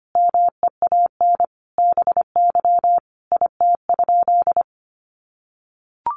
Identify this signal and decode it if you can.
Morse code, heard as a keyed tone, about 25 wpm, 700 Hz. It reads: MEAN BY IT?
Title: Sentences from Top 200 Words 25wpm